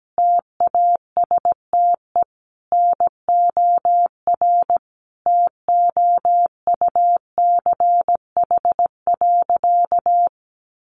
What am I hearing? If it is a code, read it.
Morse code, heard as a keyed tone, about 17 wpm, 700 Hz. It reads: TASTE NOR TOUCH.
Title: Sentences from Top 1500 Words 17wpm